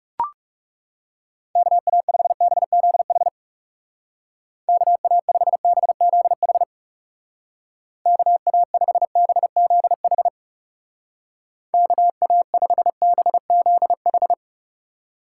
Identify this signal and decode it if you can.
Morse code, heard as a keyed tone, about 37 wpm, 700 Hz. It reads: KA5BZH KA5BZH KA5BZH KA5BZH
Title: Speed Racing - USA 2x3 Call Signs 30wpm